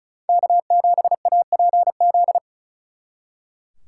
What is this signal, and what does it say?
Morse code, heard as a keyed tone, about 33 wpm, 700 Hz. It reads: K7APZ
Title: Speed Racing - USA 1x3 Call Signs 35wpm